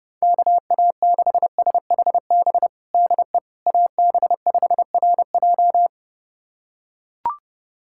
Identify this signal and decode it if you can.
Morse code, heard as a keyed tone, about 30 wpm, 700 Hz. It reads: KA6SHB DE AB5RJ
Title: USA Callsign DE Callsign 30wpm